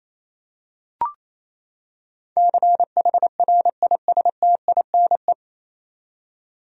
Morse code, heard as a keyed tone, about 28 wpm, 700 Hz. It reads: CHRISTINE
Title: USA Common Names - 28wpm